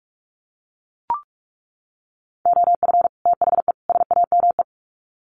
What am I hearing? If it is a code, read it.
Morse code, heard as a keyed tone, about 44 wpm, 700 Hz. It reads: OF THE SAME
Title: Sets of 3 Words 45wpm